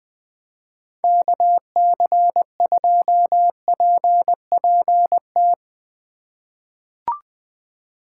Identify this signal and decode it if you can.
Morse code, heard as a keyed tone, about 20 wpm, 700 Hz. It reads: KC2PPT
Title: USA Call Signs - 6 Characters - 3x Character Spacing - 20wpm